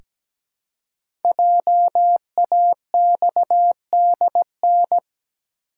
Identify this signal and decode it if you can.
Morse code, heard as a keyed tone, about 17 wpm, 700 Hz. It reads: JAXDN
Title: Code Groups - 5 Letters - 17wpm